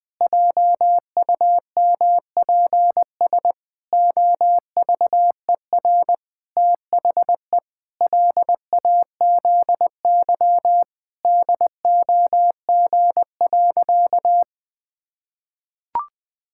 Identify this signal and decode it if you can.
Morse code, heard as a keyed tone, about 20 wpm, 700 Hz. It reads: JUMPS OVER THE LAZY DOG.
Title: Sentences with 8-Word Phrases - 20wpm